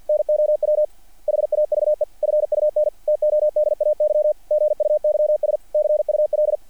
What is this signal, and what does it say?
Morse code, heard as a keyed tone, about 48 wpm, 600 Hz. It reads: NOW HAVE FUN TODAY GUYS CUL
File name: archived / CRSnet